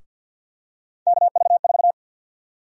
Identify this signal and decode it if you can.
Morse code, heard as a keyed tone, about 52 wpm, 700 Hz. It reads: KV4
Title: USA Call Sign Prefixes - Repeated 2x - 50wpm